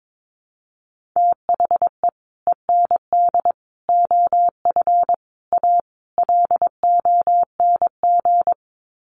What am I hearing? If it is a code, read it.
Morse code, heard as a keyed tone, about 22 wpm, 700 Hz. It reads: THE END OF A LONG
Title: Sets of 5 Words Encore2 22wpm